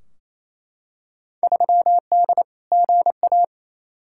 Morse code, heard as a keyed tone, about 28 wpm, 700 Hz. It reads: 3D GA
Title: Field Day Exchanges Encore - 28wpm